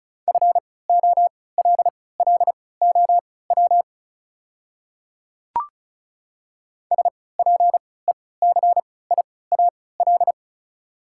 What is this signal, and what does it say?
Morse code, heard as a keyed tone, about 34 wpm, spaced as for 20 wpm, 700 Hz. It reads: FOLLOW SPECIAL
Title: 3x Character Spacing - Top 500 Words - 35wpm